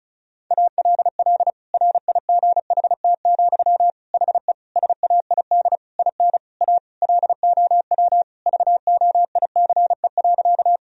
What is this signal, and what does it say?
Morse code, heard as a keyed tone, about 36 wpm, 700 Hz. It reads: ALL RIGHT, HE SAID IN A LOW VOICE.
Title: Sentences from Top 1000 Words - Review Part 1 - 35wpm